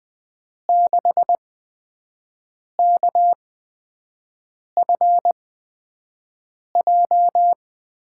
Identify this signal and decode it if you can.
Morse code, heard as a keyed tone, about 20 wpm, spaced as for 5 wpm, 700 Hz. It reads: 6KFJ
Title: USA Call Signs - 5 Characters - 8x Character Spacing - 20wpm